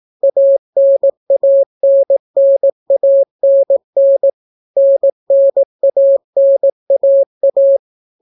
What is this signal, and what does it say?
Morse code, heard as a keyed tone, about 18 wpm, 550 Hz. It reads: ANANNANN NNANAA